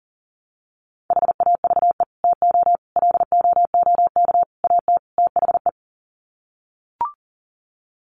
Morse code, heard as a keyed tone, about 40 wpm, 700 Hz. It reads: HAVE TO LOOK AT THE
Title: Sets of 5 Words Encore 40wpm